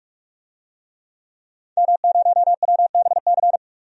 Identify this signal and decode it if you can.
Morse code, heard as a keyed tone, about 43 wpm, 700 Hz. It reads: M0WBC
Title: DXCC-100 Call Signs - 5 Characters - 45wpm